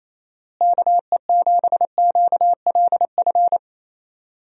Morse code, heard as a keyed tone, about 27 wpm, 700 Hz. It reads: KE7QLF
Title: USA Call Signs - 6 Characters - Repeated 2x - 28wpm